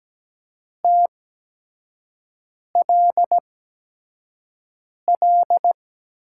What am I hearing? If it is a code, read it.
Morse code, heard as a keyed tone, about 17 wpm, spaced as for 5 wpm, 700 Hz. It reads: TLL
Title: USA Call Signs - 6 Characters - 8x Character Spacing - 17wpm